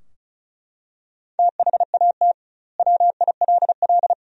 Morse code, heard as a keyed tone, about 36 wpm, 700 Hz. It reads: THAT WILL
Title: 2x Word-Spacing - Sentences from Top 100 Words - 35wpm